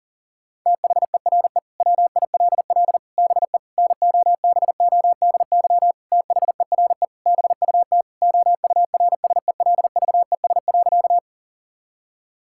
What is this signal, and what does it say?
Morse code, heard as a keyed tone, about 40 wpm, 700 Hz. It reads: THERE WILL BE NOBODY THERE BUT OURSELVES.
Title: Sentences from Top 1600 Words 40wpm